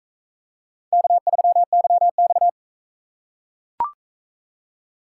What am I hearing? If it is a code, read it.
Morse code, heard as a keyed tone, about 37 wpm, 700 Hz. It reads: K3YX